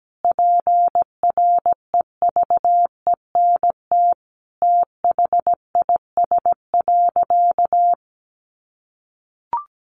Morse code, heard as a keyed tone, about 17 wpm, 700 Hz. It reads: PREVENT THIS.
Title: Sentences from Top 1100 Words 17wpm